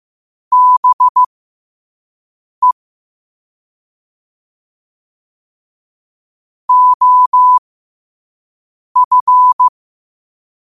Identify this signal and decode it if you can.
Morse code, heard as a keyed tone, about 15 wpm, spaced as for 5 wpm, 1000 Hz. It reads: BE OF
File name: qotdfiles / 5wpm